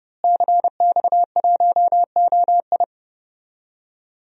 Morse code, heard as a keyed tone, about 31 wpm, 700 Hz. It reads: CX1OI
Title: DXCC-100 Call Signs - 5 Characters - Repeated 2x - 30wpm